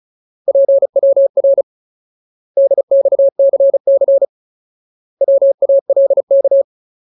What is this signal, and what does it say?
Morse code, heard as a keyed tone, about 34 wpm, 550 Hz. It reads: PWR DXCC WALK
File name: Ultimate Word List / 35WPM Ultimate